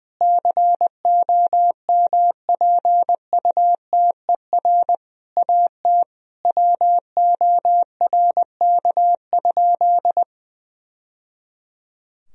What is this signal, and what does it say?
Morse code, heard as a keyed tone, about 20 wpm, 700 Hz. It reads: COMPUTER AT WORK?
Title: Sentences from Top 1000 Words - Review Part 3 - 20wpm